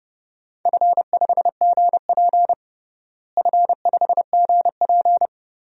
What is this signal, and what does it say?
Morse code, heard as a keyed tone, about 30 wpm, 700 Hz. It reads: F5GP F5GP
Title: DXCC-100 Call Signs - 4 Characters - Repeated 2x - 30wpm